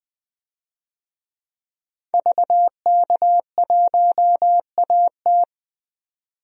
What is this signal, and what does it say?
Morse code, heard as a keyed tone, about 20 wpm, 700 Hz. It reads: VK1AT
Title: DXCC-100 Call Signs - 5 Characters - 20wpm